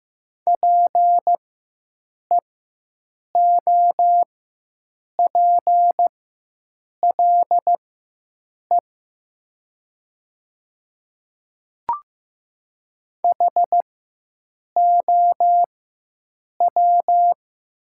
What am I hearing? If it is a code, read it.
Morse code, heard as a keyed tone, about 15 wpm, spaced as for 7 wpm, 700 Hz. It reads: PEOPLE HOW
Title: 4x Character Spacing - Top 100 Words - 15wpm